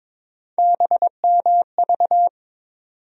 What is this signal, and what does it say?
Morse code, heard as a keyed tone, about 22 wpm, 700 Hz. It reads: BMV
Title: USA Call Sign Postfixes - 3 Letters 22wpm